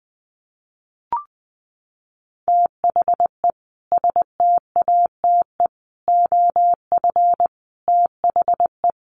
Morse code, heard as a keyed tone, about 20 wpm, 700 Hz. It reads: THE STATE OF THE
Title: Sentences from Top 1400 Words 20wpm